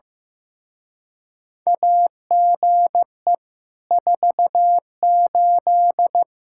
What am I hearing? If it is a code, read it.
Morse code, heard as a keyed tone, about 15 wpm, 700 Hz. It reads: AGE 48
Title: Speed-Racing Common QSO Phrases Encore - 15wpm